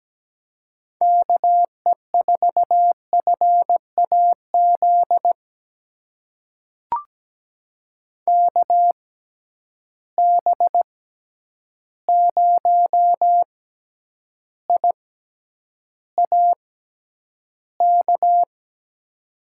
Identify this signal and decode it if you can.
Morse code, heard as a keyed tone, about 17 wpm, spaced as for 9 wpm, 700 Hz. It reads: KE4FAZ KB0IAK
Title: USA Call Signs - 6 Characters - 6x Character Spacing - 17wpm